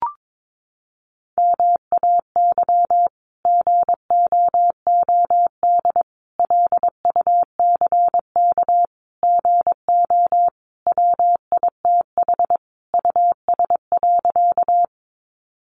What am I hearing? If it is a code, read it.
Morse code, heard as a keyed tone, about 22 wpm, 700 Hz. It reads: MAY GOOD LUCK GO WITH US.
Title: Sentences from Top 1500 Words 22wpm